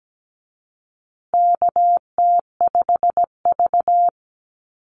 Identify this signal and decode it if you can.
Morse code, heard as a keyed tone, about 17 wpm, 700 Hz. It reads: KT5V
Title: Call Signs Repeated 2x - 17wpm